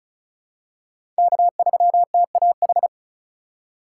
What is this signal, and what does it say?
Morse code, heard as a keyed tone, about 34 wpm, 700 Hz. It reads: K3TAH
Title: USA Call Signs - 5 Characters - 4x Character Spacing - 35wpm